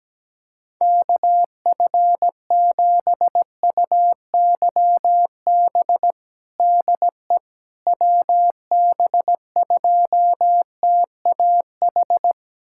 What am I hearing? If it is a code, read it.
Morse code, heard as a keyed tone, about 17 wpm, 700 Hz. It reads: KF7UYB DE WB2TAH
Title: USA Callsign DE Callsign - 6 Characters - 17wpm